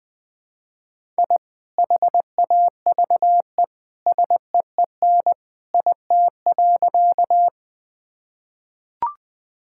Morse code, heard as a keyed tone, about 20 wpm, 700 Hz. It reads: I HAVE SEEN IT.
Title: Sentences with 2-Word Phrases - 20wpm